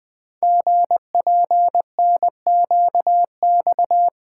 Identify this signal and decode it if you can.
Morse code, heard as a keyed tone, about 20 wpm, 700 Hz. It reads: GPNQX
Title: Code Groups - 5 Letter:Numbers - 20wpm